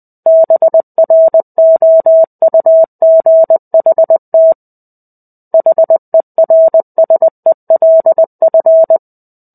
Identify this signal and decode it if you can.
Morse code, heard as a keyed tone, about 20 wpm, 650 Hz. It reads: BROUGHT HERSELF
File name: Chapter-05 20 WPM